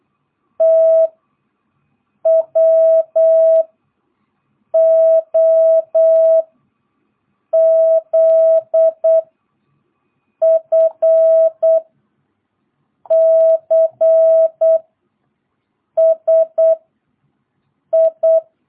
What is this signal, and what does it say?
Morse code, heard as a keyed tone, about 8 wpm, 650 Hz. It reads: T W O Z F C S I
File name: jh8cbh / 2022cw